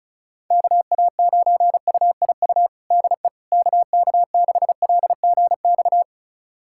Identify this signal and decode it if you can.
Morse code, heard as a keyed tone, about 36 wpm, 700 Hz. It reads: KA9UIU DE KK6LGX
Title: USA Callsign DE Callsign - 6 Characters - 35wpm